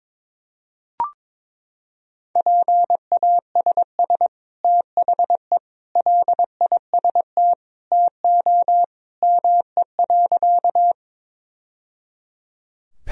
Morse code, heard as a keyed tone, about 22 wpm, 700 Hz. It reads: PASS THE LIST TO ME.
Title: Sentences from Top 1000 Words - Review Part 2 - 22wpm